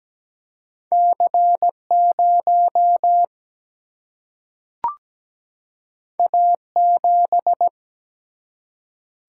Morse code, heard as a keyed tone, about 17 wpm, 700 Hz. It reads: C 0 A 7
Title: Single-Letter Single-Number 17wpm